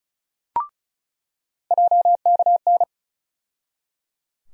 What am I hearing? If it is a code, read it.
Morse code, heard as a keyed tone, about 36 wpm, 700 Hz. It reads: JKN
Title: USA Call Sign Postfixes - 3 Letters 35wpm